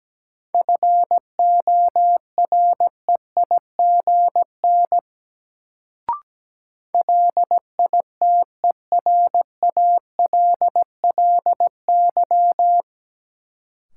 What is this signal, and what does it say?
Morse code, heard as a keyed tone, about 17 wpm, 700 Hz. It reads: FOREIGN LITERALLY
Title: Sentences from Top 1400 Words - New Word Drill - 17wpm